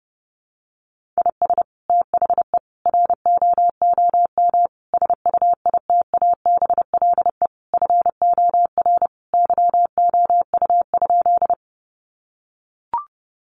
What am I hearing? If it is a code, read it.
Morse code, heard as a keyed tone, about 30 wpm, 700 Hz. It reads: IS THE ROOM SUITABLE FOR YOU?
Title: Sentences from Top 1400 Words 30wpm